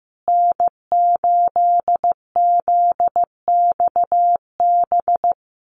Morse code, heard as a keyed tone, about 15 wpm, 700 Hz. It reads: N8ZXB
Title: USA Call Signs Repeated 3x 15wpm